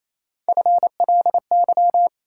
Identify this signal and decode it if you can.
Morse code, heard as a keyed tone, about 28 wpm, 700 Hz. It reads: FLY
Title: Sentences from Top 500 Words - New Word Drill - 28wpm